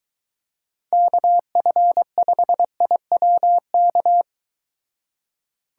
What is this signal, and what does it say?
Morse code, heard as a keyed tone, about 23 wpm, 700 Hz. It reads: KF5IWK